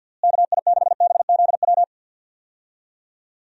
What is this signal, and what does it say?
Morse code, heard as a keyed tone, about 51 wpm, 700 Hz. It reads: KI6BZW
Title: USA Call Signs - 6 Characters - Repeated 3x - 50wpm